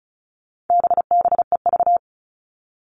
Morse code, heard as a keyed tone, about 34 wpm, 700 Hz. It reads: BBEV
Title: All Cut Numbers - 4-Digit 35wpm